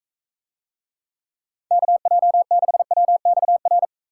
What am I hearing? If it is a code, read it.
Morse code, heard as a keyed tone, about 44 wpm, 700 Hz. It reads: KJ6WXR